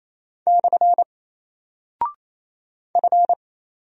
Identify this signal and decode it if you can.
Morse code, heard as a keyed tone, about 27 wpm, 700 Hz. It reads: / F
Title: Single Character (Letter:Number:Punctuation:Prosign) Alt 28wpm